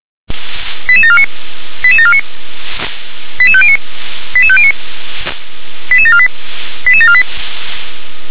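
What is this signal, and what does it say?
Five-tone selective call (ZVEI), heard as quick runs of five tones, tones of about 70 ms. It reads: 88749 88749 88509 88509 88748 88748
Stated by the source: ZVEI five-tone selective call received by SDR, address 88747.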